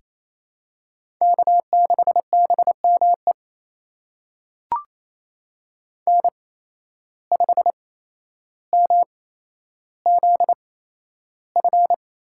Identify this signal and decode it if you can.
Morse code, heard as a keyed tone, about 28 wpm, spaced as for 12 wpm, 700 Hz. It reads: K6BME N5MZF
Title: USA Call Signs - 5 Characters - 8x Character Spacing - 28wpm